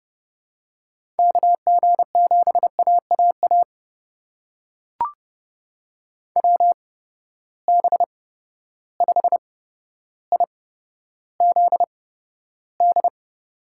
Morse code, heard as a keyed tone, about 30 wpm, spaced as for 13 wpm, 700 Hz. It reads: KG7AAA WB5IZD
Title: USA Call Signs - 6 Characters - 8x Character Spacing - 30wpm